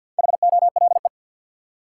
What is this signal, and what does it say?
Morse code, heard as a keyed tone, about 51 wpm, 700 Hz. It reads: HOLE
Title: Four Letter Words - 50wpm